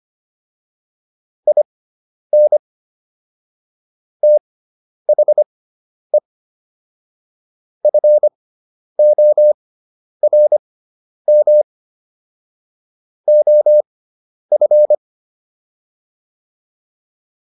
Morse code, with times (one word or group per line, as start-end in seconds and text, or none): 1.5-2.6 IN
4.2-6.2 THE
7.8-11.6 FORM
13.3-14.9 OF